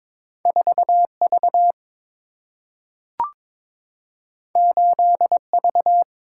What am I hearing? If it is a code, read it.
Morse code, heard as a keyed tone, about 22 wpm, 700 Hz. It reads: 4 V 8 V
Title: Single-Letter Single-Number 22wpm